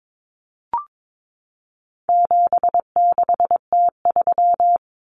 Morse code, heard as a keyed tone, about 22 wpm, 700 Hz. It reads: 76T3
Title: Common Cut Numbers - 4-Digit 22wpm